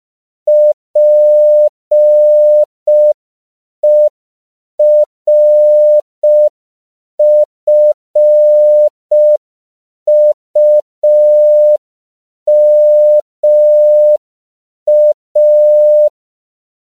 Morse code, heard as a keyed tone, about 5 wpm, 600 Hz. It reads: PERFUMA
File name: 5PPM